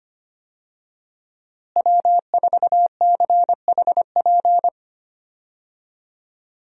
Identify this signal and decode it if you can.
Morse code, heard as a keyed tone, about 25 wpm, 700 Hz. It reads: W4CHP
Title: USA Call Signs - 5 Characters - Repeated 2x - 25wpm